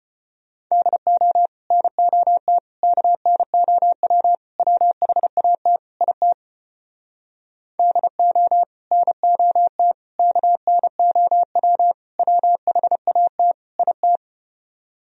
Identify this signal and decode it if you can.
Morse code, heard as a keyed tone, about 33 wpm, 700 Hz. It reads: DO NOT KNOW WHAT IT DO NOT KNOW WHAT IT